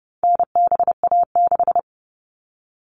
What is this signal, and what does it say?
Morse code, heard as a keyed tone, about 30 wpm, 700 Hz. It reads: NBA6
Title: All Cut Numbers - 4-Digit 30wpm